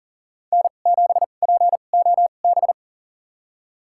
Morse code, heard as a keyed tone, about 40 wpm, spaced as for 29 wpm, 700 Hz. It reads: N7POB
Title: USA Call Signs - 5 Characters - 2x Character Spacing - 40wpm